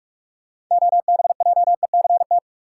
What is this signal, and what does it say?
Morse code, heard as a keyed tone, about 45 wpm, 700 Hz. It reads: OBJECT
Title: Sentences from Top 500 Words - New Word Drill - 45wpm